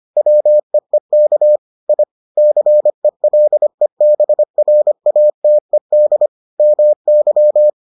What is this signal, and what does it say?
Morse code, heard as a keyed tone, about 25 wpm, 600 Hz. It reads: WEEK I CELEBRATED MY